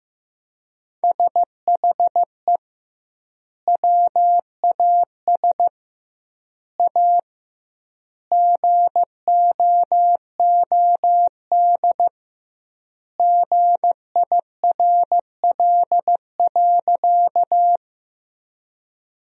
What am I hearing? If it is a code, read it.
Morse code, heard as a keyed tone, about 15 wpm, 700 Hz. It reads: SHE WAS A GOOD GIRL.
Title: 2x Word-Spacing - Sentences from Top 400 Words - 15wpm